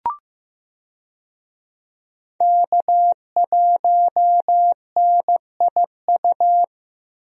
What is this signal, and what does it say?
Morse code, heard as a keyed tone, about 15 wpm, 700 Hz. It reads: K1NIU
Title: USA Call Signs - 5 Characters - Repeated 2x - 15wpm